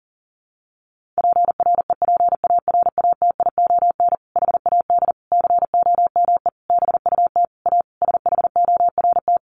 Morse code, heard as a keyed tone, about 40 wpm, 700 Hz. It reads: PREPARATION HAD COME BUT A SHORT
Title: Sentences from Top 1400 Words 40wpm